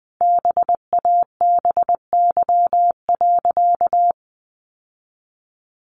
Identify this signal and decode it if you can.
Morse code, heard as a keyed tone, about 19 wpm, 700 Hz. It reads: BABY.
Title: Sentences from Top 1400 Words 20wpm